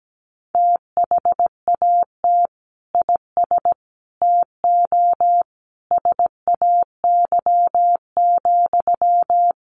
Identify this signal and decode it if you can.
Morse code, heard as a keyed tone, about 17 wpm, 700 Hz. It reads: THAT IS TO SAY,
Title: Sentences from Top 700 Words 17wpm